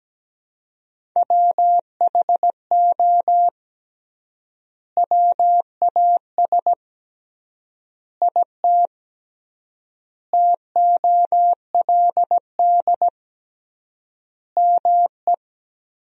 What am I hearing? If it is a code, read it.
Morse code, heard as a keyed tone, about 17 wpm, 700 Hz. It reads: WHO WAS IT TOLD ME
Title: 3x Word-Spacing - Sentences from Top 400 Words - 17wpm